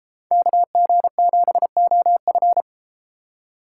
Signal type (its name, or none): Morse code